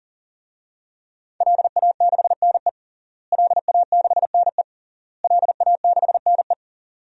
Morse code, heard as a keyed tone, about 40 wpm, 700 Hz. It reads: LA6NE LA6NE LA6NE
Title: DXCC-100 Call Signs - 5 Characters - Repeated 3x - 40wpm